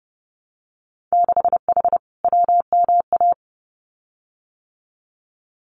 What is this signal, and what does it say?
Morse code, heard as a keyed tone, about 30 wpm, 700 Hz. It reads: 6H WMA
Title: Winter Field Day Exchanges Encore 30wpm